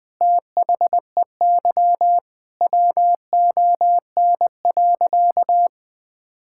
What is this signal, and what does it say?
Morse code, heard as a keyed tone, about 20 wpm, 700 Hz. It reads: THEY WON.
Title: Sentences with 2-Word Phrases - Spoken Beginning - 20wpm